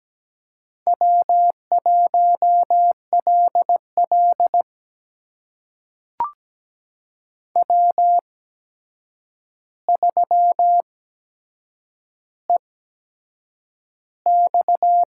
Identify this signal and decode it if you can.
Morse code, heard as a keyed tone, about 17 wpm, spaced as for 7 wpm, 700 Hz. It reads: W1LL W3EX
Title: USA Call Signs - 4 Characters - 8x Character Spacing - 17wpm